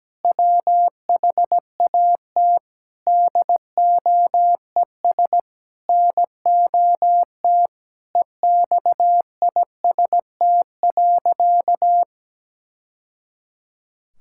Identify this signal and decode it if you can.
Morse code, heard as a keyed tone, about 17 wpm, 700 Hz. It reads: WHAT DOES NOT EXIST.
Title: Sentences from Top 1000 Words - Review Part 5 - 17wpm